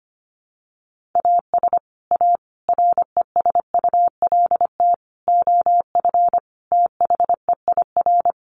Morse code, heard as a keyed tone, about 25 wpm, 700 Hz. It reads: AS A RESULT OF THEIR
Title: Sets of 5 Words Encore2 25wpm